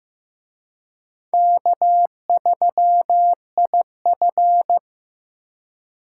Morse code, heard as a keyed tone, about 18 wpm, 700 Hz. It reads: K3IF
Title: Speed Racing - USA 1x2 Call Signs 15wpm